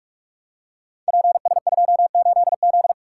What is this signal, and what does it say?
Morse code, heard as a keyed tone, about 44 wpm, 700 Hz. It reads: PS28Z
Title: Code Groups - 5 Letter:Numbers - 45wpm